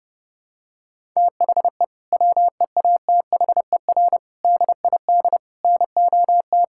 Morse code, heard as a keyed tone, about 30 wpm, 700 Hz. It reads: THE WEATHER DID NOT
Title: Sentences from Top 1000 Words - Review Part 5 - 30wpm